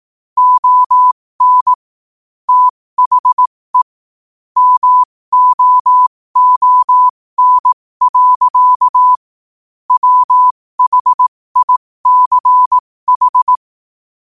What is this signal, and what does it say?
Morse code, heard as a keyed tone, about 18 wpm, spaced as for 15 wpm, 1000 Hz. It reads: ON THE MOON. WHICH
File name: qotdfiles / 15wpm